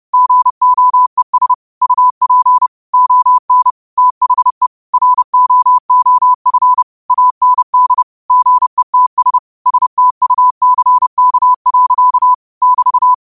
Morse code, heard as a keyed tone, about 30 wpm, 1000 Hz. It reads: GOES UP ON THE ROOF AND GETS STUCK. <BT>
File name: qotdfiles / 30wpm